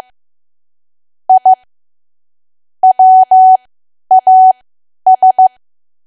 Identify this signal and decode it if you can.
Morse code, heard as a keyed tone, about 15 wpm, spaced as for 10 wpm, 750 Hz. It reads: I WAS
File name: Archive / 10 WPM / 10WPM